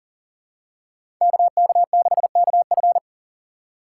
Morse code, heard as a keyed tone, about 40 wpm, 700 Hz. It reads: KK6KF
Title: USA Call Signs - 5 Characters - 4x Character Spacing - 40wpm